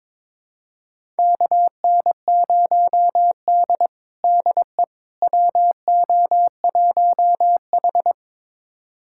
Morse code, heard as a keyed tone, about 22 wpm, 700 Hz. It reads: KN0D DE WO1H
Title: USA Callsign DE Callsign - 4 Characters - 22wpm